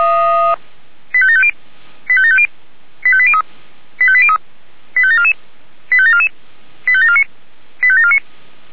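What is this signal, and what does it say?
Five-tone selective call (ZVEI), heard as quick runs of five tones, tones of about 70 ms. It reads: 85750 85750 85803 85803 85744 85744 85749 85749
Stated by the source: ZVEI five-tone selective call received on SDR.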